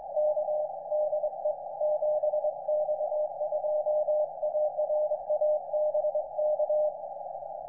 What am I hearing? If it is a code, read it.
Morse code, heard as a keyed tone, about 23 wpm, 610 Hz, with noise. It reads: K DE 7X2ARA BK
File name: iw0ffk / 2010 06 29 7X2ARA IW0FFK TR